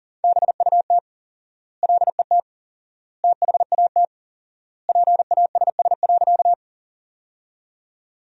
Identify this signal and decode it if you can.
Morse code, heard as a keyed tone, about 40 wpm, 700 Hz. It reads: BUT LET THAT PASS.
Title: 4x Word-Spacing - Sentences from Top 400 Words - 40wpm